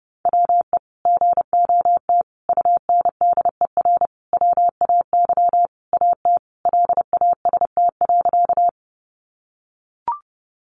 Morse code, heard as a keyed tone, about 30 wpm, 700 Hz. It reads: WE GOT UNDER WAY AT LAST.
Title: Sentences from Top 300 Words 30wpm